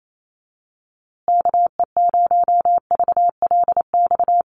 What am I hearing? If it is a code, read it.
Morse code, heard as a keyed tone, about 28 wpm, 700 Hz. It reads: KE0VLX
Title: USA Call Signs Repeated 3x 28wpm